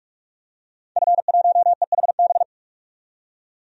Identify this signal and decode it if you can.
Morse code, heard as a keyed tone, about 41 wpm, 700 Hz. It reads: F1EHB
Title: DXCC-100 Call Signs - 5 Characters - 45wpm